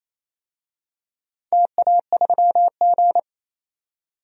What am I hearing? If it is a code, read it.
Morse code, heard as a keyed tone, about 28 wpm, 700 Hz. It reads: TA3G
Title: DXCC-100 Call Signs - 4 Characters - 28wpm